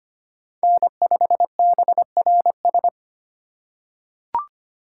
Morse code, heard as a keyed tone, about 25 wpm, 700 Hz. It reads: N5BRS
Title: USA Call Signs - 4x Character Spacing - 25wpm